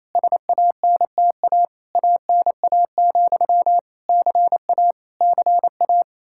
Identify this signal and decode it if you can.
Morse code, heard as a keyed tone, about 28 wpm, 700 Hz. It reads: SANTA ANA, CA CA
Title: Practice QSO - Exchange 1 - Beginning - 28wpm